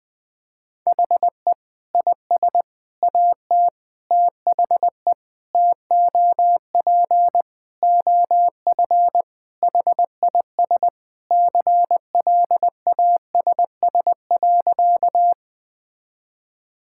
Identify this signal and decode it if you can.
Morse code, heard as a keyed tone, about 20 wpm, 700 Hz. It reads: HE IS AT THE TOP OF HIS CLASS.
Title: Sentences with 4-Word Phrases - 20wpm